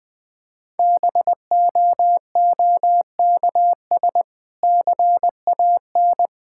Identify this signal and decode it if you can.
Morse code, heard as a keyed tone, about 20 wpm, 700 Hz. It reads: BOOKS CAN
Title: Difficult Sentences with Hints - 20wpm